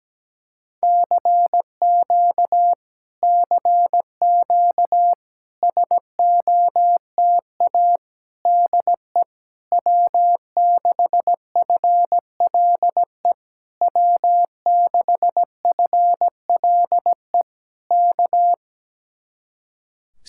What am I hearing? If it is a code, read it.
Morse code, heard as a keyed tone, about 17 wpm, 700 Hz. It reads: CQ CQ SOTA DE W6FLE W6FLE K
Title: Speed-Racing Common QSO Phrases - 17wpm